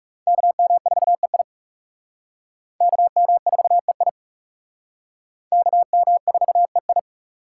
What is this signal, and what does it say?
Morse code, heard as a keyed tone, about 36 wpm, 700 Hz. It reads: KM4EI KM4EI KM4EI
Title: Speed Racing - USA 2x2 Call Signs 30wpm